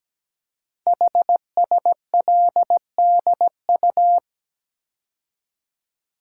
Morse code, heard as a keyed tone, about 17 wpm, 700 Hz. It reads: HSLDU
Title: Code Groups - 5 Letter:Numbers - 17wpm